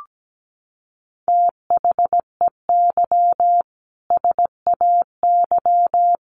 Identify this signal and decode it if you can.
Morse code, heard as a keyed tone, about 17 wpm, 700 Hz. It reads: THEY SAY
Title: Sentences from Top 1100 Words 17wpm